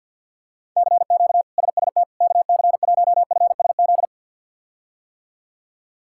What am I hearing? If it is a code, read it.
Morse code, heard as a keyed tone, about 48 wpm, 700 Hz. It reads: CQ SST KC1FSZ
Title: K1USN SST QSO Practice - 50wpm